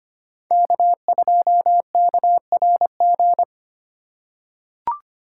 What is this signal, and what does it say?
Morse code, heard as a keyed tone, about 25 wpm, 700 Hz. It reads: K2KRG
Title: USA Call Signs - 5 Characters - 3x Character Spacing - 25wpm